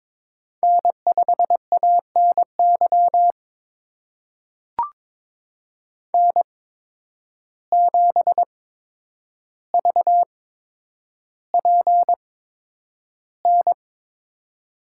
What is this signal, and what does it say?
Morse code, heard as a keyed tone, about 22 wpm, spaced as for 10 wpm, 700 Hz. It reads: N5ANY N7VPN
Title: USA Call Signs - 5 Characters - 8x Character Spacing - 22wpm